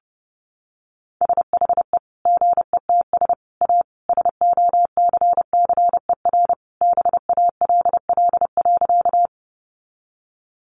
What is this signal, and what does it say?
Morse code, heard as a keyed tone, about 30 wpm, 700 Hz. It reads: SHE GETS A SOCCER BALL.
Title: Easy Phrases 30wpm